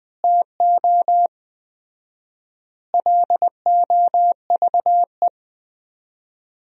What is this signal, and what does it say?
Morse code, heard as a keyed tone, about 19 wpm, 700 Hz. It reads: TO LOVE
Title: 4x Word-Spacing - Sentences from Top 400 Words - 20wpm